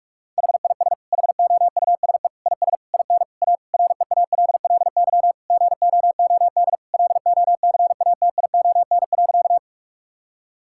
Morse code, heard as a keyed tone, about 46 wpm, 700 Hz. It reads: HIS HOUSE IS IN A REALLY GOOD LOCATION.
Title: Sentences from Top 1000 Words - Review Part 6 - 45wpm